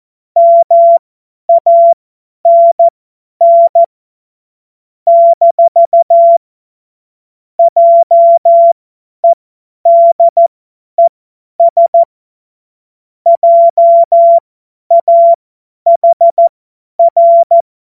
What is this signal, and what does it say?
Morse code, heard as a keyed tone, about 14 wpm, spaced as for 10 wpm, 680 Hz. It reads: MANN - JEDES JAHR
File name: Morsekurs / Witze / Witz 18. Juli 2023 50 F70